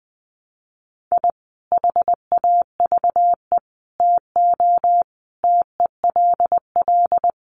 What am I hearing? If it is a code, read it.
Morse code, heard as a keyed tone, about 20 wpm, 700 Hz. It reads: I HAVE TO TELL
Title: Sets of 5 Words 20wpm